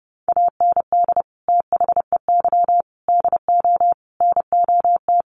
Morse code, heard as a keyed tone, about 30 wpm, 700 Hz. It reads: AND THEY DO NOT
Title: Sets of 5 Words Encore2 30wpm